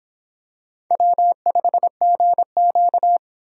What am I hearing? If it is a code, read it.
Morse code, heard as a keyed tone, about 24 wpm, 700 Hz. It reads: W5GQ